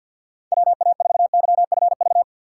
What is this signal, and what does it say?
Morse code, heard as a keyed tone, about 52 wpm, 700 Hz. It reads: WA4YFV
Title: USA Call Signs - 2x Character Spacing - 50wpm